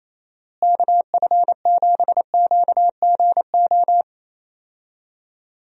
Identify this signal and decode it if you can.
Morse code, heard as a keyed tone, about 28 wpm, 700 Hz. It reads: KF7QGO
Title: USA Call Signs - 6 Characters - Repeated 2x - 28wpm